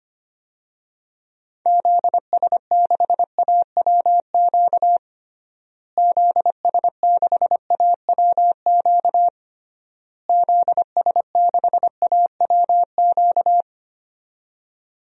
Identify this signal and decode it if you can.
Morse code, heard as a keyed tone, about 25 wpm, 700 Hz. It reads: ZS6AWQ ZS6AWQ ZS6AWQ
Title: DXCC-100 Call Signs - 6 Characters - Repeated 3x - 25wpm